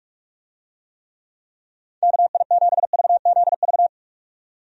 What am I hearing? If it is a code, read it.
Morse code, heard as a keyed tone, about 46 wpm, 700 Hz. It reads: KI7VZV
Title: USA Call Signs - 6 Characters - 45wpm